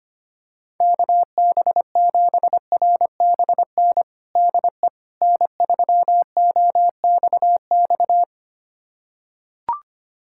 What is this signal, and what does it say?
Morse code, heard as a keyed tone, about 25 wpm, 700 Hz. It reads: KB7RBN DE N3OXX
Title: USA Callsign DE Callsign 25wpm